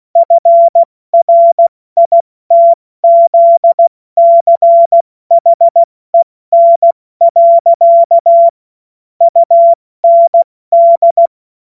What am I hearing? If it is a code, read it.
Morse code, heard as a keyed tone, about 16 wpm, spaced as for 14 wpm, 680 Hz. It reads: FRITZCHEN. UND
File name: Morsekurs / Witze / Witz 13. September 2022 70 F80